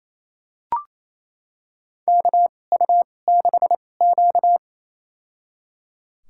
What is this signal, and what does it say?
Morse code, heard as a keyed tone, about 28 wpm, spaced as for 20 wpm, 700 Hz. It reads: KU6Q
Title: USA Call Signs - 4 Characters - 2x Character Spacing - 28wpm